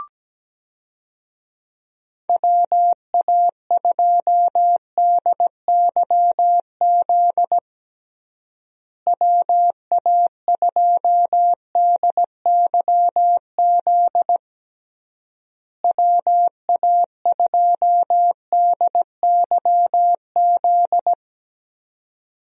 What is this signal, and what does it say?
Morse code, heard as a keyed tone, about 17 wpm, 700 Hz. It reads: WA2DYZ WA2DYZ WA2DYZ
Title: USA Call Signs - 6 Characters - Repeated 3x - 17wpm